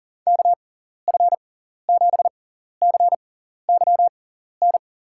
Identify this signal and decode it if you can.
Morse code, heard as a keyed tone, about 40 wpm, spaced as for 14 wpm, 700 Hz. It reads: KF7CYN
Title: USA Call Signs - 6 Characters - 6x Character Spacing - 40wpm